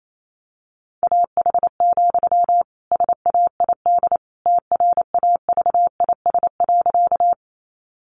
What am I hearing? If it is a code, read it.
Morse code, heard as a keyed tone, about 28 wpm, 700 Hz. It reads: AH, SAID TRAVIS.
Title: A Sound Of Thunder 28wpm